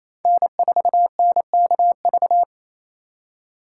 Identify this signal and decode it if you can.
Morse code, heard as a keyed tone, about 28 wpm, 700 Hz. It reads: N4NKV
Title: USA Call Signs - 4x Character Spacing - 28wpm